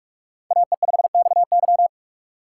Morse code, heard as a keyed tone, about 44 wpm, 700 Hz. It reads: AE5XY
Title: USA Call Signs - 5 Characters - 4x Character Spacing - 45wpm